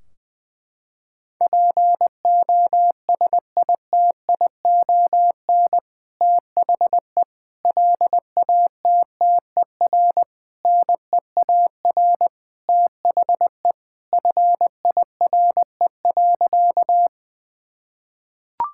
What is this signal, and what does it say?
Morse code, heard as a keyed tone, about 20 wpm, 700 Hz. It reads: POSITION THE LATTER NEAR THE FIRE.
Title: Sentences from Top 1600 Words 20wpm